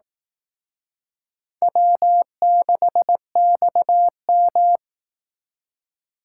Morse code, heard as a keyed tone, about 19 wpm, 700 Hz. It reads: W6XM